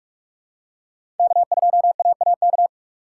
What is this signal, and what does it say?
Morse code, heard as a keyed tone, about 42 wpm, 700 Hz. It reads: K2AAK
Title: USA Call Signs - 5 Characters - 3x Character Spacing - 45wpm